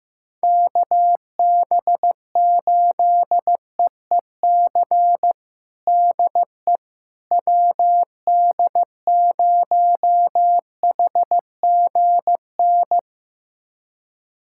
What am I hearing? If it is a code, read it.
Morse code, heard as a keyed tone, about 16 wpm, 700 Hz. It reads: KB8EEC DE WD0HGN
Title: USA Callsign DE Callsign - 6 Characters - 15wpm